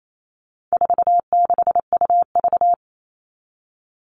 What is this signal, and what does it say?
Morse code, heard as a keyed tone, about 28 wpm, 700 Hz. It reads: 46UV
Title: All Cut Numbers - 4-Digit 28wpm